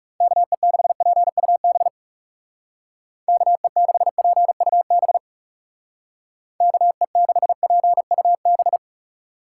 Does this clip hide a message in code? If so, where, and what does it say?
Yes: Morse code from 0.2 to 8.8 s, heard as a keyed tone, about 38 wpm, 700 Hz. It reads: KE6PUB KE6PUB KE6PUB